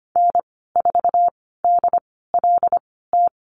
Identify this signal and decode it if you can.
Morse code, heard as a keyed tone, about 25 wpm, spaced as for 16 wpm, 700 Hz. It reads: N4DLT
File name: Lesson 15g - Number 5 - Call Signs 25wpm